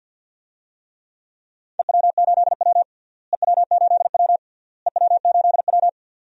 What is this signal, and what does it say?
Morse code, heard as a keyed tone, about 47 wpm, 700 Hz. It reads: EW8W EW8W EW8W
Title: DXCC-100 Call Signs - 4 Characters - Repeated 3x - 50wpm